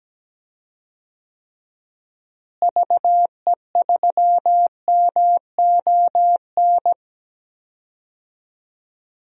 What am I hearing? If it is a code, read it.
Morse code, heard as a keyed tone, about 17 wpm, 700 Hz. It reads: VE3MON
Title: DXCC-100 Call Signs - 6 Characters - Repeated 3x - 17wpm